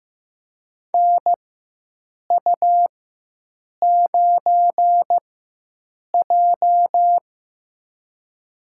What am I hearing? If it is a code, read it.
Morse code, heard as a keyed tone, about 15 wpm, spaced as for 7 wpm, 700 Hz. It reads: NU9J
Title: USA Call Signs - 4 Characters - 4x Character Spacing - 15wpm